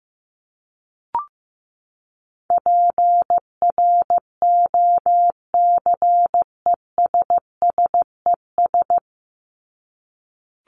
Morse code, heard as a keyed tone, about 15 wpm, 700 Hz. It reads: PROCESSES
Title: ES:S Suffix - Top 100 Words with Base Words 15wpm